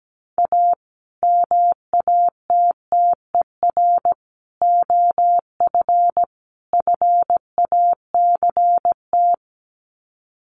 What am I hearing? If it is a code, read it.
Morse code, heard as a keyed tone, about 17 wpm, 700 Hz. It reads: A MATTER OF FACT
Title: Sets of 4 Words 17wpm